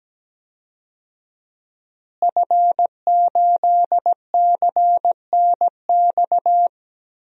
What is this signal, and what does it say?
Morse code, heard as a keyed tone, about 17 wpm, 700 Hz. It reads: F8CNX
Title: DXCC-100 Call Signs - 5 Characters - Repeated 2x - 17wpm